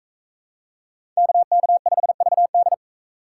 Morse code, heard as a keyed tone, about 42 wpm, 700 Hz. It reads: KK5VD